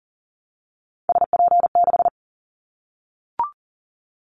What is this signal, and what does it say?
Morse code, heard as a keyed tone, about 40 wpm, 700 Hz. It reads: SP6
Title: Callsign Prefixes for Top 25 DXCC - Repeated 3x - 40wpm